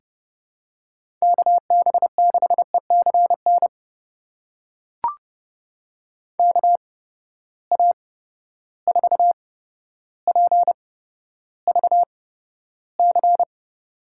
Morse code, heard as a keyed tone, about 32 wpm, spaced as for 13 wpm, 700 Hz. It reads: KB6ECN KA4PVC
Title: USA Call Signs - 6 Characters - 8x Character Spacing - 30wpm